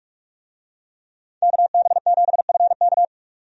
Morse code, heard as a keyed tone, about 46 wpm, 700 Hz. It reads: KB7FK
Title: USA Call Signs - 5 Characters - 45wpm